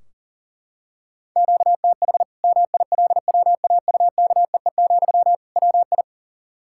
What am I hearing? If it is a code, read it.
Morse code, heard as a keyed tone, about 40 wpm, 700 Hz. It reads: QTH MILWAUKEE, WI
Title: Practice QSO - Exchange 1 - Beginning - 40wpm